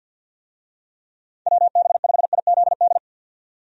Morse code, heard as a keyed tone, about 53 wpm, 700 Hz. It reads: WB5IZD
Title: USA Call Signs - 6 Characters - 50wpm